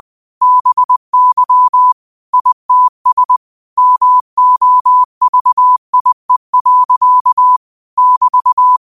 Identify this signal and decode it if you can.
Morse code, heard as a keyed tone, about 20 wpm, 1000 Hz. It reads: BY ITS MOVIE. <BT>
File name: qotdfiles / 20wpm